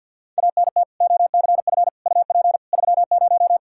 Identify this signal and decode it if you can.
Morse code, heard as a keyed tone, about 53 wpm, 700 Hz. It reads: ANT OCF UP 30
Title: Practice QSO - Exchange 2 - Info - 50wpm